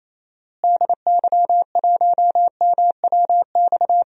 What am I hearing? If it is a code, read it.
Morse code, heard as a keyed tone, about 28 wpm, 700 Hz. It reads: DY1MWX
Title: DXCC-100 Call Signs - 6 Characters - Repeated 3x - 28wpm